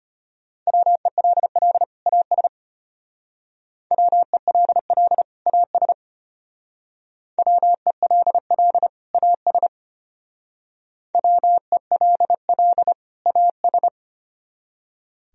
Morse code, heard as a keyed tone, about 32 wpm, 700 Hz. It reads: WELL AS WELL AS WELL AS WELL AS